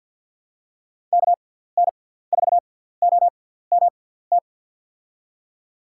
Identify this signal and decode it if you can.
Morse code, heard as a keyed tone, about 52 wpm, spaced as for 17 wpm, 700 Hz. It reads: KN4OMT
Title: USA Call Signs - 6 Characters - 6x Character Spacing - 50wpm